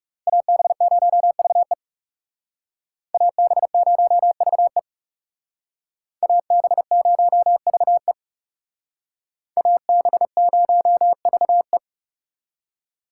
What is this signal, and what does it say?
Morse code, heard as a keyed tone, about 37 wpm, 700 Hz. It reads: AB0VE AB0VE AB0VE AB0VE
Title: Speed Racing - USA Call Signs 30wpm